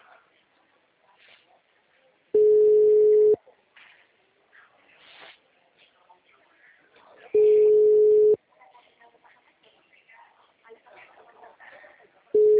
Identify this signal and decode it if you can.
Ringback tone (Europe, 425 Hz), with speech.
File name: out- -2009- - - .62279